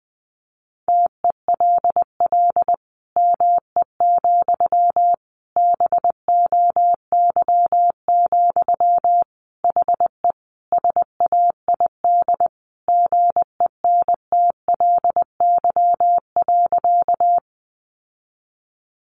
Morse code, heard as a keyed tone, about 20 wpm, 700 Hz. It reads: TELL ME, BOY, HE SAID GENTLY.
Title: Sentences from Top 1500 Words 20wpm